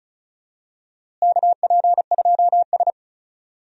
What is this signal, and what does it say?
Morse code, heard as a keyed tone, about 36 wpm, 700 Hz. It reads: KP2S
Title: DXCC-100 Call Signs - 4 Characters - Repeated 3x - 35wpm